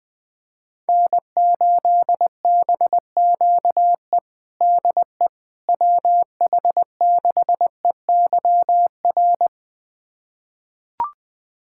Morse code, heard as a keyed tone, about 21 wpm, 700 Hz. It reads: N8BQE DE WH6EYR
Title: USA Callsign DE Callsign 20wpm